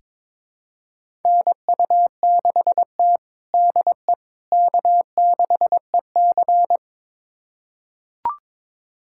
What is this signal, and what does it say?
Morse code, heard as a keyed tone, about 22 wpm, 700 Hz. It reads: NU6T DE K6EC
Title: USA Callsign DE Callsign - 4 Characters - 22wpm